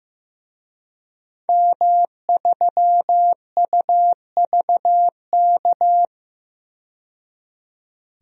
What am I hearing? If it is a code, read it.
Morse code, heard as a keyed tone, about 15 wpm, 700 Hz. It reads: M3UVK
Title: DXCC-100 Call Signs - 5 Characters - 15wpm